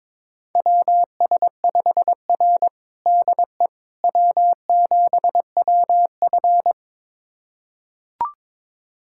Morse code, heard as a keyed tone, about 22 wpm, 700 Hz. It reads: WS5R DE W7WF
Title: USA Callsign DE Callsign - 4 Characters - 22wpm